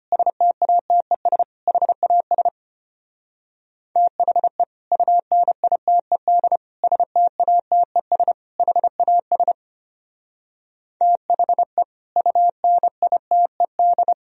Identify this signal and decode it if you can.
Morse code, heard as a keyed tone, about 32 wpm, 700 Hz. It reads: STATES HAS THE UNITED STATES HAS THE UNITED